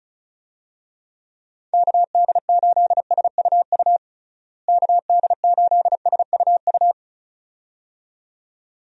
Morse code, heard as a keyed tone, about 34 wpm, 700 Hz. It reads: KD8SUU KD8SUU
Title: USA Call Signs - 6 Characters - Repeated 2x - 35wpm